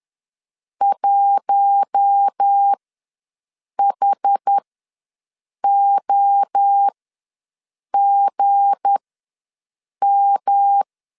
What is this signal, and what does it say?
Morse code, heard as a keyed tone, about 10 wpm, spaced as for 6 wpm, 800 Hz. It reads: 1HOGM
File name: LEZIONI / 7 GF9Z / 7 GF9Z ws 10wpm 3919